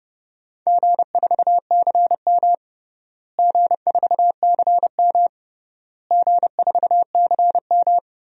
Morse code, heard as a keyed tone, about 32 wpm, 700 Hz. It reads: G4CM G4CM G4CM
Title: DXCC-100 Call Signs - 4 Characters - Repeated 3x - 30wpm